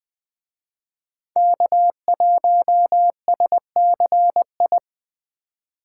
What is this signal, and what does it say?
Morse code, heard as a keyed tone, about 20 wpm, 700 Hz. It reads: K1SCI
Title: Speed Racing - USA 1x3 Call Signs 17wpm